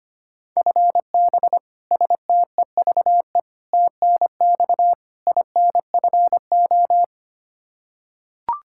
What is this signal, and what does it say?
Morse code, heard as a keyed tone, about 25 wpm, 700 Hz. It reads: FB STEVE TNX INFO
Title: Practice QSO - Exchange 3 - More Info - 25wpm